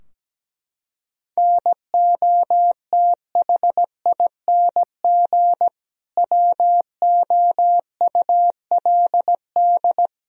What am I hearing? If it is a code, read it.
Morse code, heard as a keyed tone, about 17 wpm, 700 Hz. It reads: NOTHING WOULD
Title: Sentences from Top 1000 Words - Review Part 3 - 17wpm